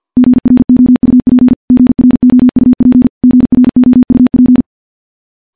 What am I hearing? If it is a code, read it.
Morse code, heard as a keyed tone, about 51 wpm, 250 Hz. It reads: GROUP GROUP GROUP
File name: morse / morse-training-50WPM